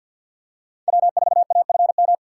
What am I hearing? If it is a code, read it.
Morse code, heard as a keyed tone, about 50 wpm, 700 Hz. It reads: W4AFM
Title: USA Call Signs - 6x Character Spacing - 50wpm